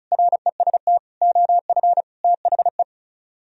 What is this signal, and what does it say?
Morse code, heard as a keyed tone, about 43 wpm, 700 Hz. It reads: REST OF THE
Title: Speed-Racing - Sets of 5 Words 35wpm